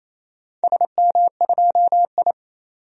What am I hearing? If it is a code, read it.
Morse code, heard as a keyed tone, about 27 wpm, 700 Hz. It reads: SM2I
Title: DXCC-100 Call Signs - 4 Characters - 28wpm